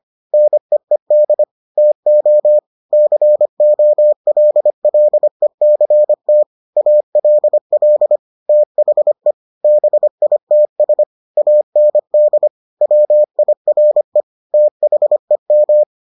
Morse code, heard as a keyed tone, about 25 wpm, 600 Hz. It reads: NEED TO COLLECT ALL THE BITS AND WIRE THEM